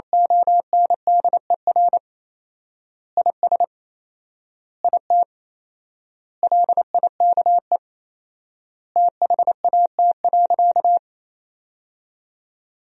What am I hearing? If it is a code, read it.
Morse code, heard as a keyed tone, about 28 wpm, 700 Hz. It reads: ONDER IS IT LIKE THAT.
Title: 4x Word-Spacing - Sentences from Top 500 Words - 28wpm